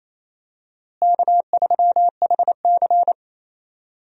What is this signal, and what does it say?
Morse code, heard as a keyed tone, about 28 wpm, 700 Hz. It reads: K3HC
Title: USA Call Signs - 4 Characters - Repeated 3x - 28wpm